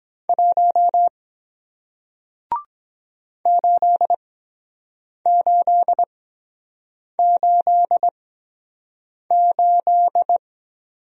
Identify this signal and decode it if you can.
Morse code, heard as a keyed tone, about 22 wpm, spaced as for 7 wpm, 700 Hz. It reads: 1 8888